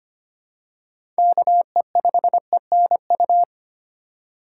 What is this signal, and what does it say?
Morse code, heard as a keyed tone, about 25 wpm, 700 Hz. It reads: KE5ENU
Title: USA Call Signs - 2x Character Spacing - 25wpm